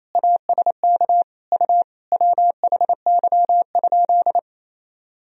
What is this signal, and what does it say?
Morse code, heard as a keyed tone, about 28 wpm, 700 Hz. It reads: ASK U WHY?
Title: Sentences from Top 200 Words and Abbreviations 28wpm